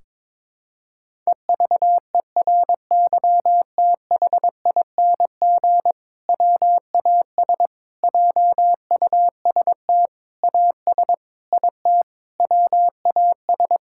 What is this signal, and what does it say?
Morse code, heard as a keyed tone, about 22 wpm, 700 Hz. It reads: EVERYTHING WAS JUST AS IT WAS
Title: Sentences from Top 1600 Words 22wpm